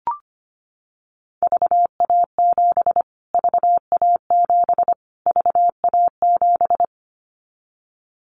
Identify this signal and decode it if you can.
Morse code, heard as a keyed tone, about 25 wpm, 700 Hz. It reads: VA7 VA7 VA7
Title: Callsign Prefixes for Top 25 DXCC - Repeated 3x - 25wpm